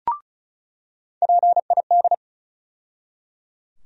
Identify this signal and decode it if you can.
Morse code, heard as a keyed tone, about 34 wpm, 700 Hz. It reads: PID
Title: USA Call Sign Postfixes - 3 Letters 35wpm